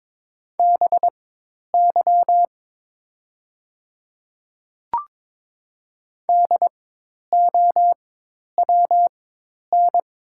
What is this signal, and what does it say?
Morse code, heard as a keyed tone, about 22 wpm, spaced as for 10 wpm, 700 Hz. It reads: BY DOWN
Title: 4x Character Spacing - Top 100 Words - 22wpm